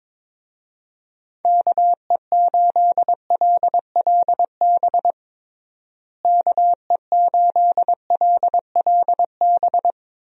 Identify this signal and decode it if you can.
Morse code, heard as a keyed tone, about 22 wpm, 700 Hz. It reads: KE8LLB KE8LLB
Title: USA Call Signs - 6 Characters - Repeated 2x - 22wpm